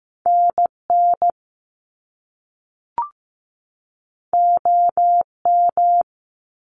Morse code, heard as a keyed tone, about 15 wpm, 700 Hz. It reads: NN OM
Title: Top Two Letter Combinations in Words 15wpm